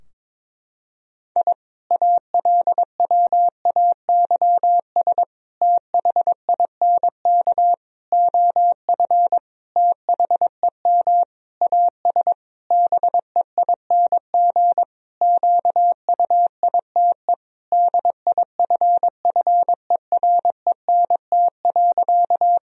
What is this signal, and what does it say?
Morse code, heard as a keyed tone, about 22 wpm, 700 Hz. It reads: I ALWAYS THINK OF THEM AS BEING QUITE DIFFERENT.
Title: Sentences from Top 1000 Words - Review Part 3 - 22wpm